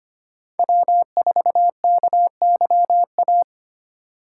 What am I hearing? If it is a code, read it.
Morse code, heard as a keyed tone, about 25 wpm, 700 Hz. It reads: W4KYA
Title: USA Call Signs - 5 Characters - 3x Character Spacing - 25wpm